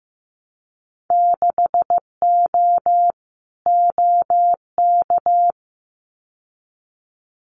Morse code, heard as a keyed tone, about 15 wpm, 700 Hz. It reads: 6O OK
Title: Winter Field Day Exchanges Encore 15wpm